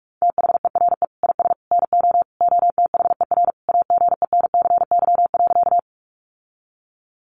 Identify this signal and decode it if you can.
Morse code, heard as a keyed tone, about 44 wpm, 700 Hz. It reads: THERE IS NO OTHER AGENCY.
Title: Sentences from Top 1300 Words 45wpm